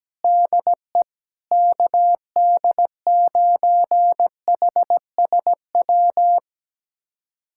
Morse code, heard as a keyed tone, about 17 wpm, 700 Hz. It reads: DE KD9HSW
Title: USA Callsign DE Callsign - 6 Characters - 17wpm